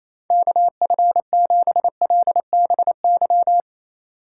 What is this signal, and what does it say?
Morse code, heard as a keyed tone, about 28 wpm, 700 Hz. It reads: KF7LBY
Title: USA Call Signs - 6 Characters - 2x Character Spacing - 28wpm